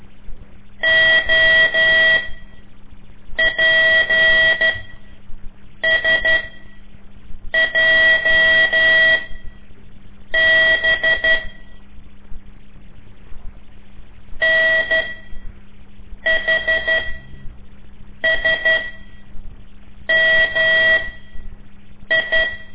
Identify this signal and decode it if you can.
Morse code, heard as a keyed tone, about 12 wpm, spaced as for 6 wpm, 1950 Hz. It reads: OPSJB NHSMI
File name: ik1weg2 / CORSI CW / lez5-1